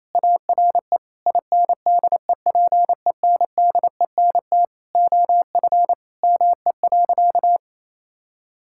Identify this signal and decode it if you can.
Morse code, heard as a keyed tone, about 28 wpm, 700 Hz. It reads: ARE INDEPENDENT OF ME.
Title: Sentences from Top 1000 Words - Review Part 6 - 28wpm